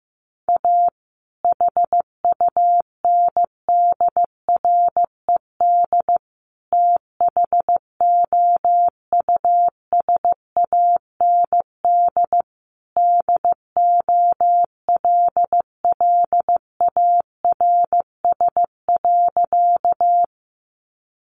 Morse code, heard as a keyed tone, about 15 wpm, 700 Hz. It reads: A HUNDRED THOUSAND DOLLARS.
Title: A Sound Of Thunder 15wpm